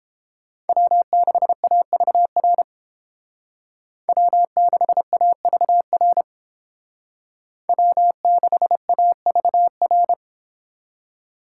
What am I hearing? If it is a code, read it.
Morse code, heard as a keyed tone, about 29 wpm, 700 Hz. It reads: W6AVR W6AVR W6AVR